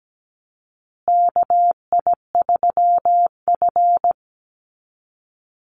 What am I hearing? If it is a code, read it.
Morse code, heard as a keyed tone, about 17 wpm, 700 Hz. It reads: KI3F
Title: Call Signs Repeated 2x - 17wpm